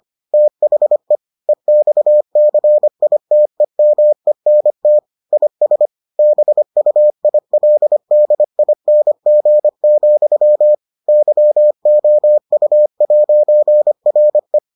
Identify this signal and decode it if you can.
Morse code, heard as a keyed tone, about 25 wpm, 600 Hz. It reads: THE EXCITEMENT IS BUILDING, YOU'RE